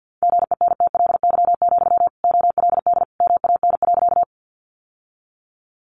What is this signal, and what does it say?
Morse code, heard as a keyed tone, about 50 wpm, 700 Hz. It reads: GENTLY, OLD MAN.
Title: Sentences from Top 1500 Words 50wpm